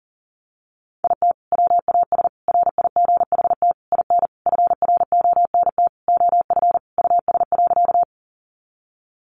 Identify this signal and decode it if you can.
Morse code, heard as a keyed tone, about 40 wpm, 700 Hz. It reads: IT WAS RIGHT IN FRONT OF US.
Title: Sentences from Top 500 Words 40wpm